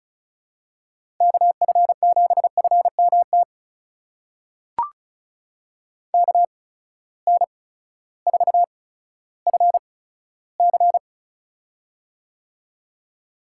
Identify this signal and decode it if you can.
Morse code, heard as a keyed tone, about 36 wpm, spaced as for 16 wpm, 700 Hz. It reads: KF7FMT KN4FC
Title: USA Call Signs - 8x Character Spacing - 35wpm